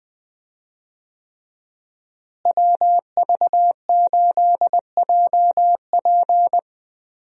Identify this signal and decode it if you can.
Morse code, heard as a keyed tone, about 20 wpm, 700 Hz. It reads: WV8JP
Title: USA Call Signs - 5 Characters - Repeated 3x - 20wpm